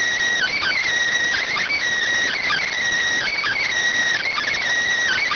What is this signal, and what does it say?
Emergency Alert System (SAME) tones (header bursts), heard as harsh buzzing bursts. It is heard from 0.3 to 4.6 s.